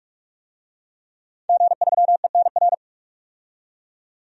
Morse code, heard as a keyed tone, about 41 wpm, 700 Hz. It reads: G3ENR
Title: DXCC-100 Call Signs - 5 Characters - 45wpm